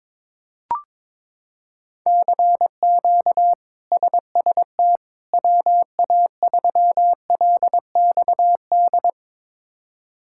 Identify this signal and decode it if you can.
Morse code, heard as a keyed tone, about 22 wpm, 700 Hz. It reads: CQ SST WA3LXD
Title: K1USN SST QSO Practice - 22wpm